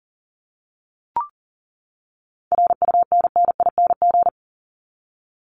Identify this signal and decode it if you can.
Morse code, heard as a keyed tone, about 38 wpm, 700 Hz. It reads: RUNNING
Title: ING Suffix - Top 100 Words with Base Words 40wpm